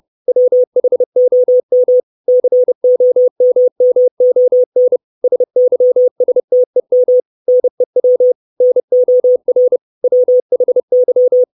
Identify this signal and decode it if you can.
Morse code, heard as a keyed tone, about 30 wpm, 500 Hz. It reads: WHOM COMMON SYSTEM NEW NOR WHY
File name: 632 Words 30WPM